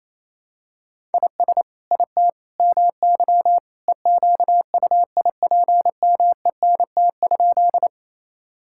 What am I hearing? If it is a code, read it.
Morse code, heard as a keyed tone, about 28 wpm, 700 Hz. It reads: IS IT MY EQUIPMENT?
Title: Sentences from Top 1000 Words - Review Part 5 - 28wpm